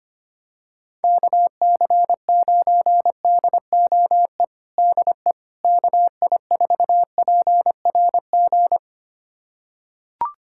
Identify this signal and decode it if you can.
Morse code, heard as a keyed tone, about 25 wpm, 700 Hz. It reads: KC9DOE DE KI4PRG
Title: USA Callsign DE Callsign - 6 Characters - 25wpm